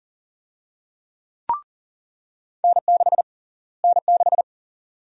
Morse code, heard as a keyed tone, about 40 wpm, 700 Hz. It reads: N6 N6
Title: USA Call Sign Prefixes - Repeated 2x - 40wpm